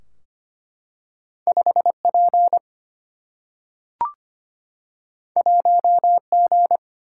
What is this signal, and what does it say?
Morse code, heard as a keyed tone, about 25 wpm, 700 Hz. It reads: 5P 1G
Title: Single-Letter Single-Number 25wpm